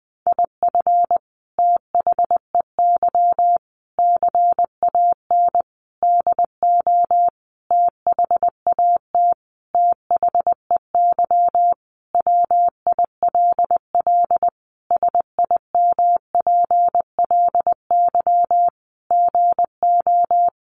Morse code, heard as a keyed tone, about 20 wpm, 700 Hz. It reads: IF THEY CAN DO THAT THEY WILL SIMPLY GO
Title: Sentences from Top 600 Words 20wpm